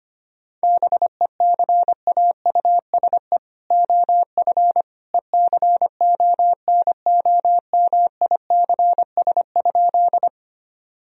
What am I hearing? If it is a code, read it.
Morse code, heard as a keyed tone, about 25 wpm, 700 Hz. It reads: BECAUSE OF ECONOMICS?
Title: Sentences from Top 1000 Words - Review Part 3 - 25wpm